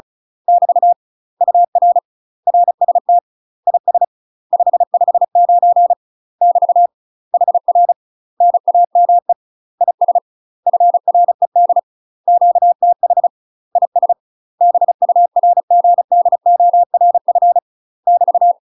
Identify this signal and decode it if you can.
Morse code, heard as a keyed tone, about 36 wpm, 700 Hz. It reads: <BT> UR RST IS 559 <BT> HR NAME IS FRED QTH IS BURGDORF <BT>
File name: UK MASTER QSOs TRAINING / 35WPM / UK MASTER QSOs 0028